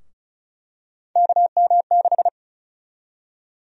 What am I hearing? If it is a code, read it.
Morse code, heard as a keyed tone, about 34 wpm, 700 Hz. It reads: KM6
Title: USA Call Sign Prefixes - 35wpm